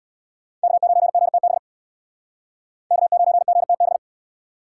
Morse code, heard as a keyed tone, about 64 wpm, 700 Hz. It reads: K9GTB K9GTB